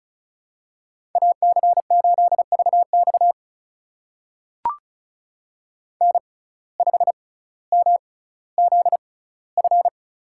Morse code, heard as a keyed tone, about 36 wpm, spaced as for 18 wpm, 700 Hz. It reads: AC8VX N5MZF
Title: USA Call Signs - 5 Characters - 6x Character Spacing - 35wpm